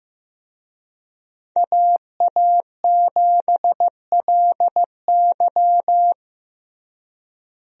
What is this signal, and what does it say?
Morse code, heard as a keyed tone, about 15 wpm, 700 Hz. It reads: AA7LY
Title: USA Call Signs - 5 Characters - Repeated 3x - 15wpm